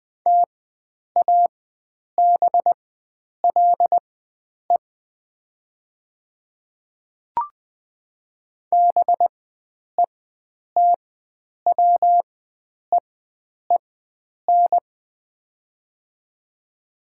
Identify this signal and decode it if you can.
Morse code, heard as a keyed tone, about 20 wpm, spaced as for 9 wpm, 690 Hz. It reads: TABLE BETWEEN
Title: 4x Character Spacing - Top 500 Words - 20wpm